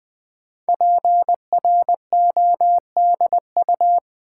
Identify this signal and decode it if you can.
Morse code, heard as a keyed tone, about 20 wpm, 700 Hz. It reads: PRODU
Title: Build A Word - 5 Letters At A Time - 10-Letter Words - 20wpm